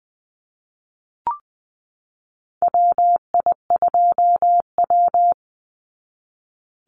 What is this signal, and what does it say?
Morse code, heard as a keyed tone, about 20 wpm, 700 Hz. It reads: WI2W
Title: NA SOTA Chaser Callsigns 20wpm